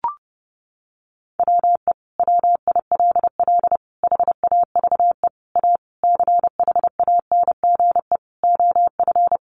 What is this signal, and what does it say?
Morse code, heard as a keyed tone, about 28 wpm, 700 Hz. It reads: WE WILL HAVE A CHANGE OF
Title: Sentences from Top 1400 Words 30wpm